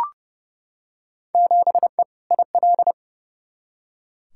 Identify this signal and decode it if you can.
Morse code, heard as a keyed tone, about 32 wpm, 700 Hz. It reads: 7E IL
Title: Field Day Exchanges Encore - 30wpm